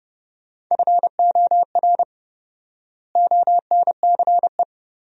Morse code, heard as a keyed tone, about 30 wpm, 700 Hz. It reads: FOR ONCE
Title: 4x Word-Spacing - Sentences from Top 300 Words - 30wpm